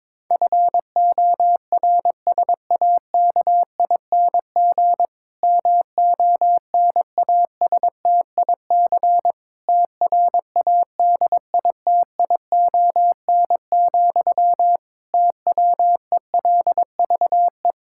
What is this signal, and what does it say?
Morse code, heard as a keyed tone, about 22 wpm, 700 Hz. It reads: FORSAKING MONASTIC TRADITION, TWELVE
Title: Pangram Challenge v2 - 22wpm